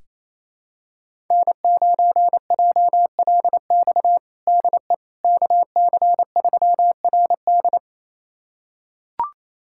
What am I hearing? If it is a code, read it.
Morse code, heard as a keyed tone, about 28 wpm, 700 Hz. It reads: N9JLX DE KC3RD
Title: USA Callsign DE Callsign 28wpm